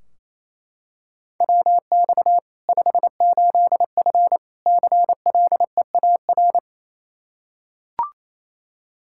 Morse code, heard as a keyed tone, about 28 wpm, 700 Hz. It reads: WX 58F CLEAR
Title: Practice QSO - Exchange 2 - Info - 28wpm